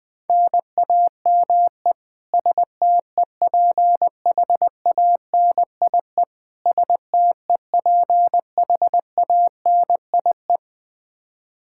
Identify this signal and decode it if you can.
Morse code, heard as a keyed tone, about 20 wpm, 700 Hz. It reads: NAME STEPHANIE STEPHANIE
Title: Practice QSO - Exchange 1 - Beginning - 20wpm